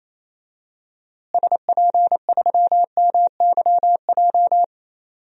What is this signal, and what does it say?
Morse code, heard as a keyed tone, about 28 wpm, 700 Hz. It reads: SP3MYJ
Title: DXCC-100 Call Signs - 6 Characters - Repeated 3x - 28wpm